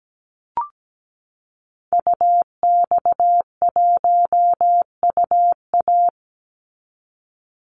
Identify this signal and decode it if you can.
Morse code, heard as a keyed tone, about 17 wpm, 700 Hz. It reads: UX1UA
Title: Call Signs Repeated 3x - 17wpm